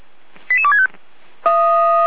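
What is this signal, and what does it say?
Five-tone selective call (ZVEI), heard as quick runs of five tones, tones of about 70 ms. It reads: 88265
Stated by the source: ZVEI five-tone selective call received on an SDR.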